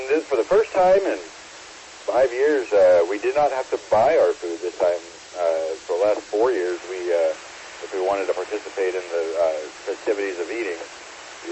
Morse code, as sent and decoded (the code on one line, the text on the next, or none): none